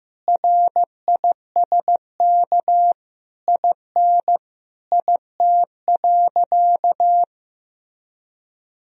Morse code, heard as a keyed tone, about 15 wpm, 700 Hz. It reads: RISK IN IT.
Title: Sentences from Top 1000 Words - Review Part 4 - 15wpm